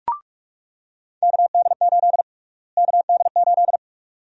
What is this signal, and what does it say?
Morse code, heard as a keyed tone, about 46 wpm, 700 Hz. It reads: KD8 KD8
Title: USA Call Sign Prefixes - Repeated 2x - 45wpm